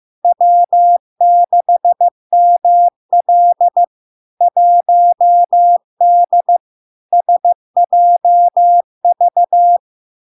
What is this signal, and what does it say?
Morse code, heard as a keyed tone, about 15 wpm, 700 Hz. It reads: W6ML 1D SJV
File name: LARK-FD-2022-Log-in-CW-15wpm-700-Hz